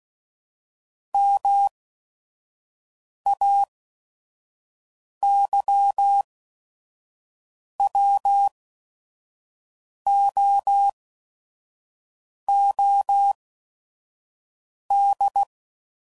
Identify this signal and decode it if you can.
Morse code, heard as a keyed tone, about 16 wpm, spaced as for 5 wpm, 800 Hz. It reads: MAYWOOD
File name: CW Code Groups @5WPM / fromSM's / Cities